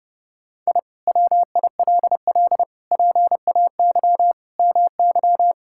Morse code, heard as a keyed tone, about 32 wpm, 700 Hz. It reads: I WILL PAY MY
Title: Sentences from Top 1000 Words - Review Part 5 - 30wpm